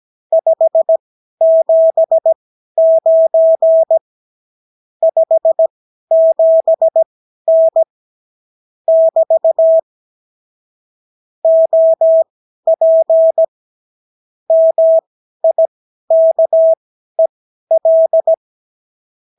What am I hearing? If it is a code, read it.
Morse code, heard as a keyed tone, about 17 wpm, spaced as for 12 wpm, 650 Hz. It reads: 579 57N <BT> OP MIKEL
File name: CW QSO libroCW4 18-8ppm